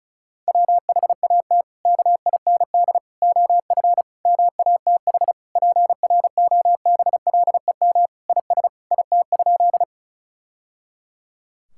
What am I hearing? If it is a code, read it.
Morse code, heard as a keyed tone, about 34 wpm, 700 Hz. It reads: WHAT KIND OF MATH PROBLEM IS IT?
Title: Sentences from Top 1000 Words - Review Part 6 - 35wpm